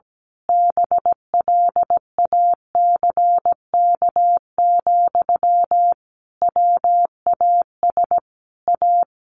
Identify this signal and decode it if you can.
Morse code, heard as a keyed tone, about 17 wpm, 700 Hz. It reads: BLACK, WAS A
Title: A Sound Of Thunder 17wpm